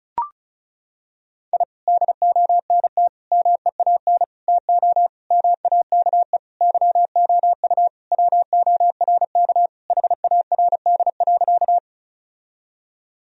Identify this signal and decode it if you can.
Morse code, heard as a keyed tone, about 34 wpm, 700 Hz. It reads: I DONT MEAN TO MAKE YOU WORK HARD.
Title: Sentences from Top 1000 Words - Review Part 1 - 35wpm